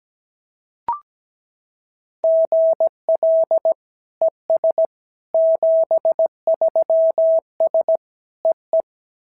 Morse code, heard as a keyed tone, about 17 wpm, 650 Hz. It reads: GL ES 73S EE
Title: SOTA QSO Practice 17wpm